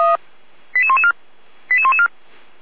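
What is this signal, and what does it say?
Five-tone selective call (ZVEI), heard as quick runs of five tones, tones of about 70 ms. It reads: 88114 88114
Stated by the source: ZVEI five-tone call 88111 received via SDR.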